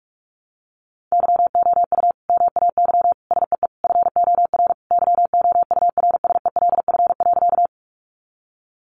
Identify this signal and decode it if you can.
Morse code, heard as a keyed tone, about 44 wpm, 700 Hz. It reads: YOU MAY SEE FOR YOURSELF.
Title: Sentences from Top 600 Words 45wpm